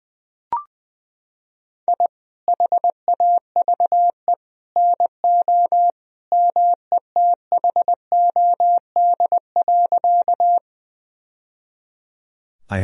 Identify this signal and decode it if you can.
Morse code, heard as a keyed tone, about 20 wpm, 700 Hz. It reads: I HAVE NO METHOD.
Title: Sentences from Top 1000 Words - Review Part 3 - 20wpm